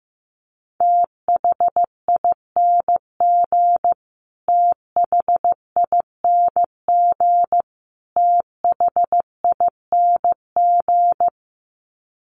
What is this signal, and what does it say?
Morse code, heard as a keyed tone, about 15 wpm, 700 Hz. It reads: THING THING THING
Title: Top 300 Words Repeated 15wpm